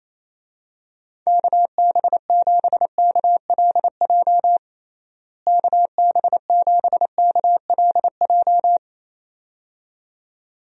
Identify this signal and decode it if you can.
Morse code, heard as a keyed tone, about 27 wpm, 700 Hz. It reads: KB7KLJ KB7KLJ
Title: USA Call Signs - 6 Characters - Repeated 2x - 28wpm